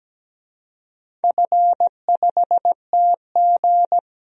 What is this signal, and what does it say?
Morse code, heard as a keyed tone, about 17 wpm, 700 Hz. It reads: F5TG
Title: DXCC-100 Call Signs - 4 Characters - Repeated 2x - 17wpm